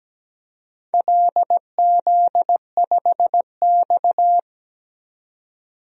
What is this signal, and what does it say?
Morse code, heard as a keyed tone, about 17 wpm, 700 Hz. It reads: LZ5X
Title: DXCC-100 Call Signs - 4 Characters - 17wpm